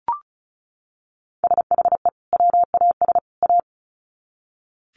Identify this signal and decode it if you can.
Morse code, heard as a keyed tone, about 34 wpm, 700 Hz. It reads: SHE WAS A
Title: Sets of 3 Words 35wpm